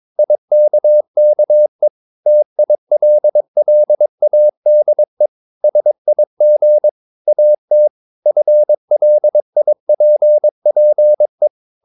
Morse code, heard as a keyed tone, about 22 wpm, 600 Hz. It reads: IKKE TILLADE SIG AT FLIPPE
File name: Ferie QRS 22wpm